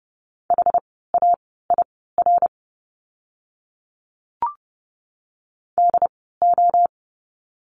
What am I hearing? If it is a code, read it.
Morse code, heard as a keyed tone, about 30 wpm, spaced as for 17 wpm, 700 Hz. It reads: HAIR DO
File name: Lesson 17f - Letter u - Words 30wpm